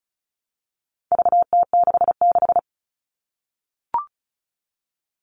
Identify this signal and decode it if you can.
Morse code, heard as a keyed tone, about 34 wpm, 700 Hz. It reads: VT66
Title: All Cut Numbers - 4-Digit 35wpm